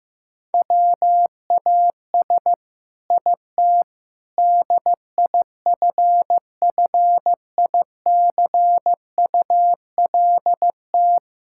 Morse code, heard as a keyed tone, about 15 wpm, 700 Hz. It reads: WAS IT DIFFICULT
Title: Sentences from Top 1000 Words - Review Part 6 - 15wpm